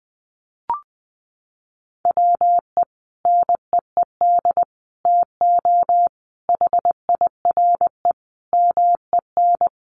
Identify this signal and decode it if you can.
Morse code, heard as a keyed tone, about 20 wpm, 700 Hz. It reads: WE NEED TO HIRE MEN
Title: Sentences from Top 1400 Words 20wpm